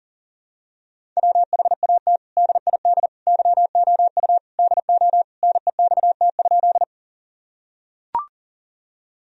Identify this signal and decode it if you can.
Morse code, heard as a keyed tone, about 37 wpm, 700 Hz. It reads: WHAT DID YOU DO NEXT?
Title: Sentences from Top 1000 Words - Review Part 1 - 40wpm